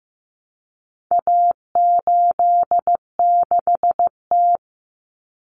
Morse code, heard as a keyed tone, about 15 wpm, 700 Hz. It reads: A86T
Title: Common Cut Numbers - 4-Digit 15wpm